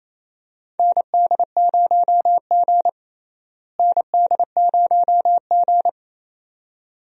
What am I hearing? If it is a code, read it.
Morse code, heard as a keyed tone, about 27 wpm, 700 Hz. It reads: ND0G ND0G
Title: USA Call Signs - 4 Characters - Repeated 2x - 28wpm